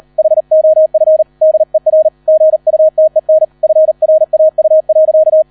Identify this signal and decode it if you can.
Morse code, heard as a keyed tone, about 38 wpm, 630 Hz. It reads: HOF DER GUTEN FRAU.